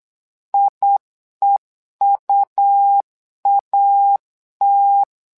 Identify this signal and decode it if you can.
Morse code, heard as a keyed tone, about 8 wpm, 800 Hz. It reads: IEUAT
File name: LEZIONI / 2 UHDN / 2 UH 8wpm 2129